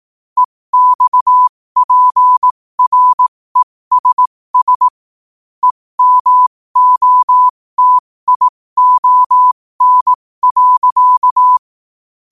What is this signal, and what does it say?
Morse code, heard as a keyed tone, about 18 wpm, spaced as for 15 wpm, 1000 Hz. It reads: EXPRESS EMOTION.
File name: qotdfiles / 15wpm